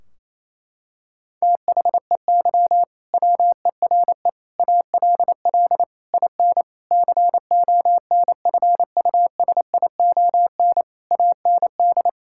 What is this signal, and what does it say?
Morse code, heard as a keyed tone, about 28 wpm, 700 Hz. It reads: THEY WERE ALL IN CONFUSION AND
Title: Sentences from Top 1600 Words 28wpm